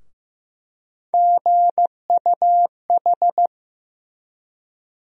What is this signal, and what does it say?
Morse code, heard as a keyed tone, about 15 wpm, 700 Hz. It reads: GUH
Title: USA Call Sign Postfixes - 3 Letters 15wpm